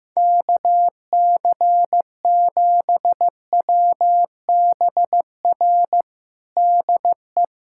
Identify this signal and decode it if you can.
Morse code, heard as a keyed tone, about 15 wpm, 700 Hz. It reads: KC7WBR DE
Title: USA Callsign DE Callsign - 6 Characters - 15wpm